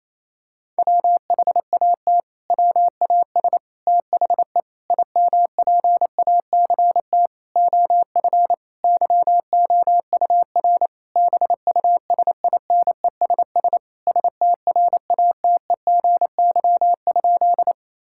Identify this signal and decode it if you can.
Morse code, heard as a keyed tone, about 28 wpm, 700 Hz. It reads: WHAT WAS THE IMPACT OF YOUR BUSINESS STRATEGY?
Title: Sentences from Top 1000 Words - Review Part 6 - 28wpm